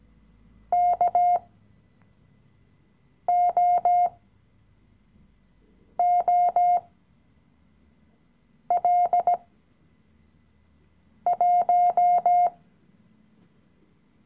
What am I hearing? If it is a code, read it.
Morse code, heard as a keyed tone, about 18 wpm, spaced as for 4 wpm, 710 Hz. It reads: KOOL1
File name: CW-Online-Kurs / Anfänger / Lektion 016